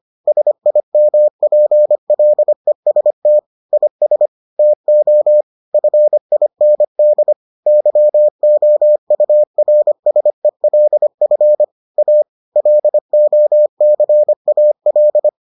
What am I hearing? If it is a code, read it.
Morse code, heard as a keyed tone, about 25 wpm, 600 Hz. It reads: SIMPLEST IS TO FIND YOURSELF A LOCAL